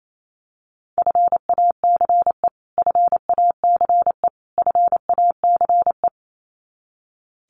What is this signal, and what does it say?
Morse code, heard as a keyed tone, about 28 wpm, 700 Hz. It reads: FACE FACE FACE
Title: Top 500 Words Repeated 28wpm